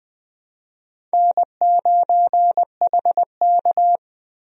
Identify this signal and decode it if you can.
Morse code, heard as a keyed tone, about 20 wpm, 700 Hz. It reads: N9HK
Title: USA Call Signs - 4 Characters - Repeated 3x - 20wpm